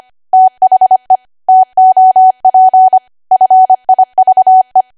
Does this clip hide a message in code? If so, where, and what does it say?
Yes: Morse code from 0.3 to 4.8 s, heard as a keyed tone, about 25 wpm, 750 Hz. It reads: THE TOP FIVE